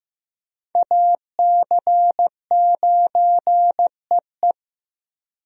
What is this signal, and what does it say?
Morse code, heard as a keyed tone, about 16 wpm, 700 Hz. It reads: AC9EE
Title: USA Call Signs - 5 Characters - Repeated 3x - 15wpm